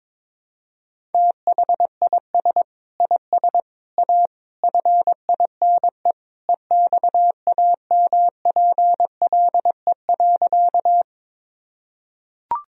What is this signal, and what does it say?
Morse code, heard as a keyed tone, about 22 wpm, 700 Hz. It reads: THIS IS A FINE EXAMPLE.
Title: Sentences from Top 1000 Words - Review Part 1 - 22wpm